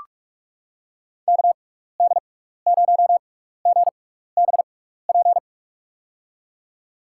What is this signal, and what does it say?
Morse code, heard as a keyed tone, about 46 wpm, spaced as for 16 wpm, 700 Hz. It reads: KD0GBP
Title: USA Call Signs - 6 Characters - 6x Character Spacing - 45wpm